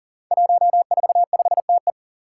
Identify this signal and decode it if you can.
Morse code, heard as a keyed tone, about 43 wpm, 700 Hz. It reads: 145TE
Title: Five Challenging Letter-Number Combinations 40wpm